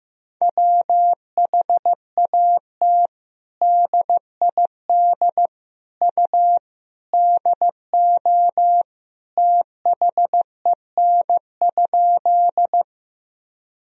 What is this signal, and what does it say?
Morse code, heard as a keyed tone, about 15 wpm, 700 Hz. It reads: WHAT DID U DO THEN?
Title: Sentences from Top 100 Words and Abbreviations 15wpm